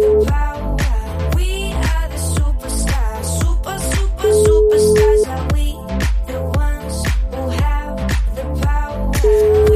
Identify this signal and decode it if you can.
Ringback tone (Europe, 425 Hz).